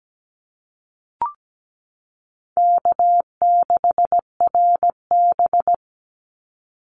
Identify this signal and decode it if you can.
Morse code, heard as a keyed tone, about 17 wpm, 700 Hz. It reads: K6RB
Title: Easy Call Signs Repeated 2x - 17wpm